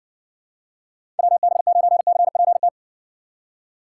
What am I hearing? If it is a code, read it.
Morse code, heard as a keyed tone, about 60 wpm, 700 Hz. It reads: WB9CPT